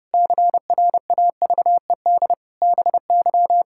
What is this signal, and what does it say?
Morse code, heard as a keyed tone, about 32 wpm, 700 Hz. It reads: CRAVED BY
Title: Pangram Challenge v2 - 30wpm